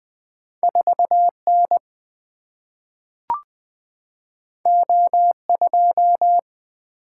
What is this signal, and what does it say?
Morse code, heard as a keyed tone, about 20 wpm, 700 Hz. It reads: 4 N O 2
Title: Single-Letter Single-Number 20wpm